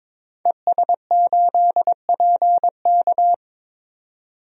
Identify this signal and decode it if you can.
Morse code, heard as a keyed tone, about 22 wpm, 700 Hz. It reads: ES8PK
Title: DXCC-100 Call Signs - 5 Characters - 22wpm